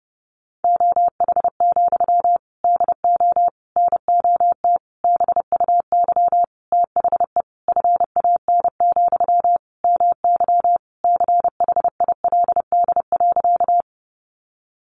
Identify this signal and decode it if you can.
Morse code, heard as a keyed tone, about 30 wpm, 700 Hz. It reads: OH, DO NOT BUY THE FAN, MY CHILD.
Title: Sentences from Top 1300 Words 30wpm